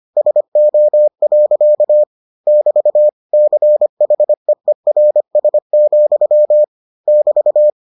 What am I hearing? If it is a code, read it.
Morse code, heard as a keyed tone, about 25 wpm, 600 Hz. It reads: SO. <BT> CHEERS, <BT>